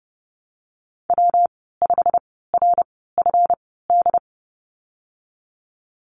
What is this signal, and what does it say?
Morse code, heard as a keyed tone, about 31 wpm, spaced as for 17 wpm, 700 Hz. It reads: W5RFD
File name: Lesson 37g - Letter x - Call Signs 30wpm